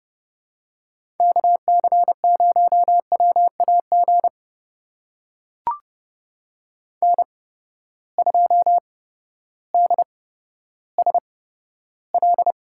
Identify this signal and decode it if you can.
Morse code, heard as a keyed tone, about 30 wpm, spaced as for 14 wpm, 700 Hz. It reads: KC0WAG N2DSL
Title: USA Call Signs - 8x Character Spacing - 30wpm